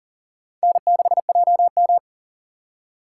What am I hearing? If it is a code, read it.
Morse code, heard as a keyed tone, about 40 wpm, 700 Hz. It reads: N6JM
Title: USA Call Signs - 4 Characters - 3x Character Spacing - 40wpm